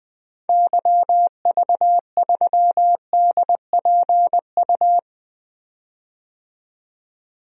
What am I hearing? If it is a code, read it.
Morse code, heard as a keyed tone, about 20 wpm, 700 Hz. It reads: YV3DPU
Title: DXCC-100 Call Signs - 6 Characters - 20wpm